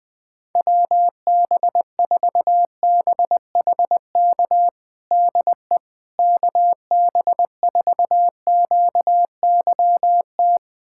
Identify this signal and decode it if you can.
Morse code, heard as a keyed tone, about 20 wpm, 700 Hz. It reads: WB4BHK DE KB4QYT
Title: USA Callsign DE Callsign - 6 Characters - 20wpm